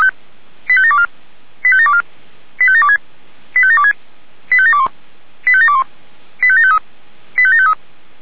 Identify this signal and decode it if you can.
Five-tone selective call (ZVEI), heard as quick runs of five tones, tones of about 70 ms. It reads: 85724 85724 85726 85726 85721 85721 85753 85753